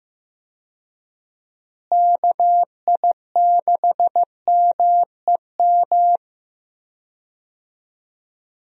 Morse code, heard as a keyed tone, about 15 wpm, 700 Hz. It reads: KI6MEM
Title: USA Call Signs - 6 Characters - Repeated 3x - 15wpm